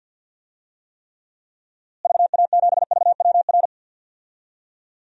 Morse code, heard as a keyed tone, about 48 wpm, 700 Hz. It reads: VA7VWR
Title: DXCC-100 Call Signs - 6 Characters - Repeated 2x - 50wpm